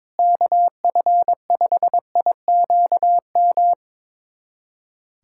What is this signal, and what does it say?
Morse code, heard as a keyed tone, about 22 wpm, 700 Hz. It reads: KF5IQM
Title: USA Callsign DE Callsign 22wpm